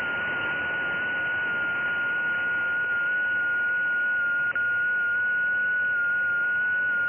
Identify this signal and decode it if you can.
FT8.
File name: IC-7100